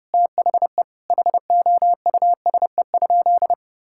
Morse code, heard as a keyed tone, about 32 wpm, 700 Hz. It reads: THE HOUSE?
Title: Sentences from Top 1000 Words - Review Part 5 - 30wpm